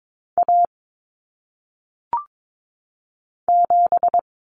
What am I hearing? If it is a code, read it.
Morse code, heard as a keyed tone, about 22 wpm, 700 Hz. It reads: A 7
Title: Common Cut Numbers - 1-Digit 22wpm